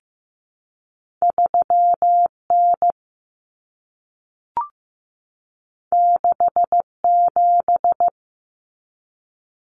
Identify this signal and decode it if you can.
Morse code, heard as a keyed tone, about 15 wpm, 700 Hz. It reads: 3N 67
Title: Common Cut Numbers - 2-Digit 15wpm